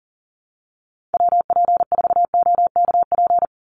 Morse code, heard as a keyed tone, about 40 wpm, 700 Hz. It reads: WP4OKP
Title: USA Call Signs 40wpm